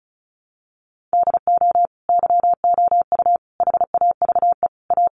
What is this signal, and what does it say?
Morse code, heard as a keyed tone, about 34 wpm, 700 Hz. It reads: DO YOU HAVE A
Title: Sentences from Top 1400 Words 35wpm